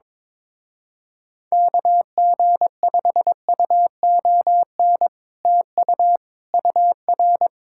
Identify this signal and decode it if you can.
Morse code, heard as a keyed tone, about 22 wpm, 700 Hz. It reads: KG5UON TU UR
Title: Speed-Racing Common QSO Phrases - 22wpm